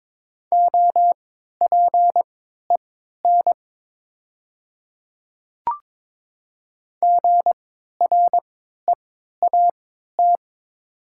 Morse code, heard as a keyed tone, about 22 wpm, spaced as for 12 wpm, 700 Hz. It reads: OPEN GREAT
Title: 3x Character Spacing - Top 300 Words - 22wpm